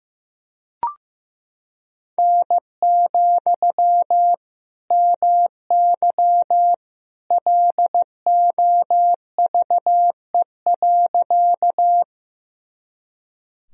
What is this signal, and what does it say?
Morse code, heard as a keyed tone, about 15 wpm, 700 Hz. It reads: N, MY LOVE.
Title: Sentences from Top 400 Words and Abbreviations 15wpm